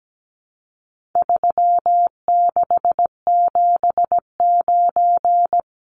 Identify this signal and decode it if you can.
Morse code, heard as a keyed tone, about 17 wpm, 700 Hz. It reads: 3679
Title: Numbers - 4-Digit 17wpm